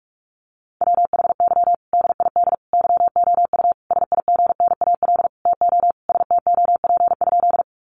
Morse code, heard as a keyed tone, about 44 wpm, 700 Hz. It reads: WHY DID YOU SIGNAL TO STOP?
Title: Sentences from Top 1400 Words 45wpm